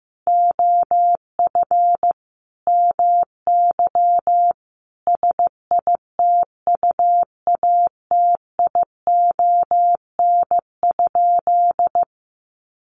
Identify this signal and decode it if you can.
Morse code, heard as a keyed tone, about 15 wpm, 700 Hz. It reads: OF MY SITUATION?
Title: Sentences from Top 1200 Words 15wpm